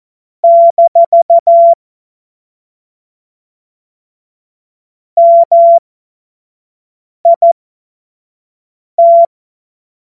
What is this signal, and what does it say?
Morse code, heard as a keyed tone, about 14 wpm, spaced as for 5 wpm, 680 Hz. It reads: - MIT
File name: Morsekurs / Witze / Witz 31. Januar 2023 25 F70